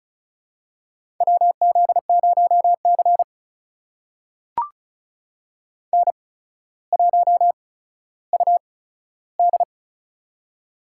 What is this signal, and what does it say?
Morse code, heard as a keyed tone, about 36 wpm, spaced as for 15 wpm, 700 Hz. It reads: WZ0C N1UD
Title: USA Call Signs - 8x Character Spacing - 35wpm